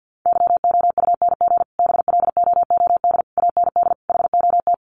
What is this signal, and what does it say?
Morse code, heard as a keyed tone, about 49 wpm, 700 Hz. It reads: YOUNG BLOOD AND HOT
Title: Sentences from Top 1100 Words 50wpm